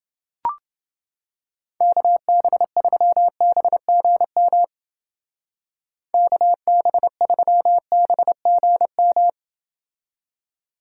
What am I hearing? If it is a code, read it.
Morse code, heard as a keyed tone, about 29 wpm, 700 Hz. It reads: KB3BGM KB3BGM